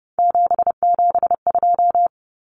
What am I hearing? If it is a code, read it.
Morse code, heard as a keyed tone, about 30 wpm, 700 Hz. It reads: 772
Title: Numbers - 3-Digit 30wpm